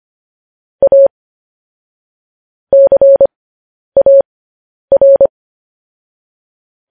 Morse code, heard as a keyed tone, about 25 wpm, spaced as for 10 wpm, 550 Hz. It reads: A CAR
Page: News Headlines in Morse Code at 10 WPM